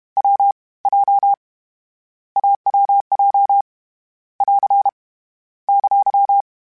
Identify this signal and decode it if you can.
Morse code, heard as a keyed tone, about 31 wpm, 800 Hz. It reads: W J AWJ <AR> !
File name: Difficult Combos / Difficult Combos-mp3s / AWJ ShortRun@30